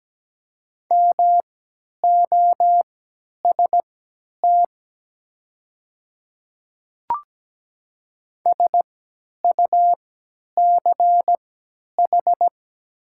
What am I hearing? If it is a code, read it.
Morse code, heard as a keyed tone, about 17 wpm, spaced as for 10 wpm, 700 Hz. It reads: MOST SUCH
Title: 3x Character Spacing - Top 200 Words - 17wpm